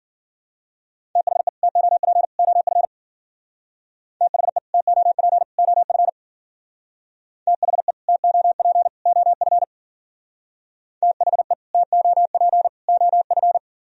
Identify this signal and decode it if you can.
Morse code, heard as a keyed tone, about 53 wpm, 700 Hz. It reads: THE TOP OF THE TOP OF THE TOP OF THE TOP OF